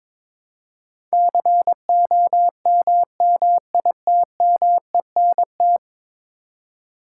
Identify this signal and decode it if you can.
Morse code, heard as a keyed tone, about 22 wpm, 700 Hz. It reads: COMMITMENT
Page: morse-code-ninja-practice-22wpm